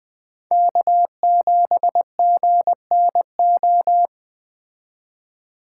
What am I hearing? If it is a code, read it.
Morse code, heard as a keyed tone, about 20 wpm, 700 Hz. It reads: K7GNO